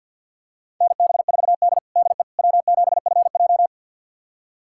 Morse code, heard as a keyed tone, about 46 wpm, 700 Hz. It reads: NB4D DE W7FJ
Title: USA Callsign DE Callsign - 4 Characters - 50wpm